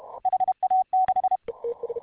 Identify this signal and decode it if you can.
Morse code, heard as a keyed tone, about 31 wpm, 750 Hz. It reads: HA6